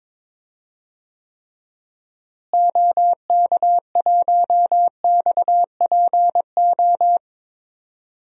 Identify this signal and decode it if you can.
Morse code, heard as a keyed tone, about 22 wpm, 700 Hz. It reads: OK1XPO
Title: DXCC-100 Call Signs - 6 Characters - Repeated 2x - 22wpm